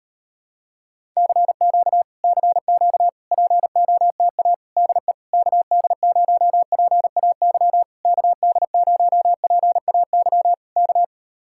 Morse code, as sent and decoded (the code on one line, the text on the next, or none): -.-. --.- / -.-. --.- / .--. --- - .- / -.. / -.- -.. ----- .--. .- -.-- / -.- -.. ----- .--. .- -.-- / -.-
CQ CQ POTA DE KD0PAY KD0PAY K